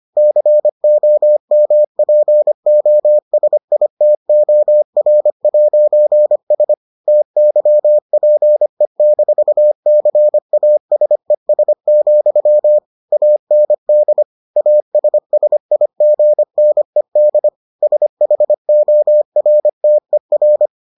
Morse code, heard as a keyed tone, about 25 wpm, 600 Hz. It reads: COMPOSITOR'S TYPE-CASES, AND ASSIGNED SHORTER